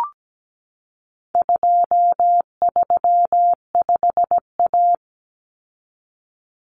Morse code, heard as a keyed tone, about 17 wpm, 700 Hz. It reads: 235A
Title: Common Cut Numbers - 4-Digit 17wpm